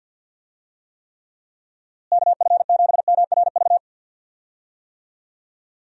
Morse code, heard as a keyed tone, about 48 wpm, 700 Hz. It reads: KF7MRV
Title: USA Call Signs - 6 Characters - Repeated 2x - 50wpm